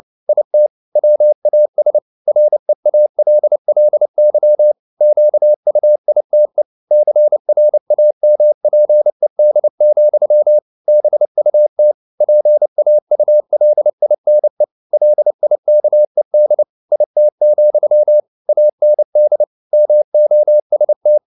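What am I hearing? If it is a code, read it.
Morse code, heard as a keyed tone, about 29 wpm, 600 Hz. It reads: IT WAS REALLY QUITE CRAMPED, BUT PAULINE LIKED IT, AND MOST
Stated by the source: Morse code at 29 wpm, sending Ken Follett's novel Never in English, from CW med Gnister.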